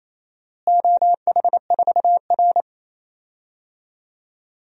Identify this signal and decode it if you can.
Morse code, heard as a keyed tone, about 28 wpm, 700 Hz. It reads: OH4R
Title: DXCC-100 Call Signs - 4 Characters - Repeated 3x - 28wpm